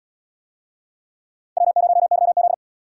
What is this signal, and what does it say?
Morse code, heard as a keyed tone, about 66 wpm, 700 Hz. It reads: W0YZ